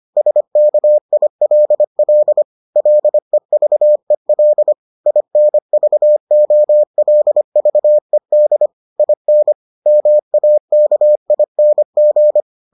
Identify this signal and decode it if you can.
Morse code, heard as a keyed tone, about 25 wpm, 600 Hz. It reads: SKILL LEVEL INVOLVED IN MAKING